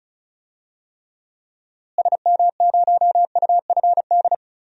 Morse code, heard as a keyed tone, about 36 wpm, 700 Hz. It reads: SM0UFD
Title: DXCC-100 Call Signs - 6 Characters - Repeated 3x - 35wpm